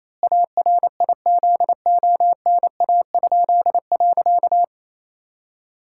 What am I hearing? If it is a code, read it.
Morse code, heard as a keyed tone, about 27 wpm, 700 Hz. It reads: ARIZONA?.
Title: Pangram Challenge v2 - 28wpm